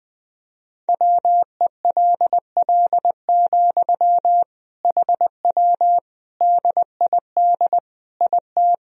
Morse code, heard as a keyed tone, about 20 wpm, 700 Hz. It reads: WELL, HW DID IT
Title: Sentences from Top 200 Words and Abbreviations 20wpm